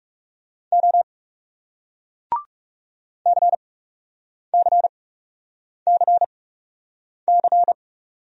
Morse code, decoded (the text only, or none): O CCCC